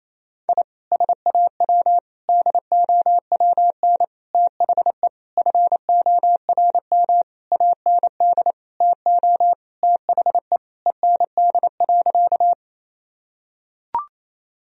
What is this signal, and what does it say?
Morse code, heard as a keyed tone, about 28 wpm, 700 Hz. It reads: I SAW DOWN THE FORM AND TO THE END.
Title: Sentences from Top 1000 Words - Review Part 1 - 28wpm